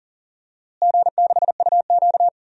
Morse code, heard as a keyed tone, about 40 wpm, 700 Hz. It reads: G6UQ
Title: DXCC-100 Call Signs - 4 Characters - Repeated 2x - 40wpm